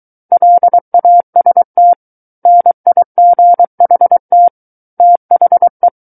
Morse code, heard as a keyed tone, about 23 wpm, 700 Hz. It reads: LAST NIGHT THE
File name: Book One / 14-Chapter 14 - 23 WPM